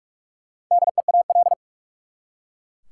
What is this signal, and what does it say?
Morse code, heard as a keyed tone, about 41 wpm, 700 Hz. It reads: DEAL
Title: Four Letter Words - 45wpm